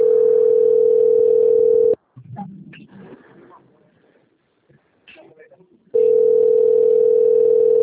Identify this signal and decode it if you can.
Ringback tone (North America), 440 + 480 Hz, with speech.
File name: out- - - - - .40460